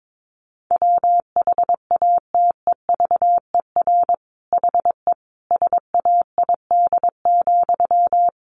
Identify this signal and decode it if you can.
Morse code, heard as a keyed tone, about 22 wpm, 700 Hz. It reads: WHATEVER HE SAID,
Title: Sentences from Top 1100 Words 22wpm